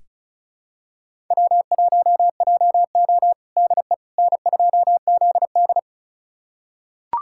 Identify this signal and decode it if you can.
Morse code, heard as a keyed tone, about 36 wpm, 700 Hz. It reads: W1JO DE N2ZD
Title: USA Callsign DE Callsign - 4 Characters - 35wpm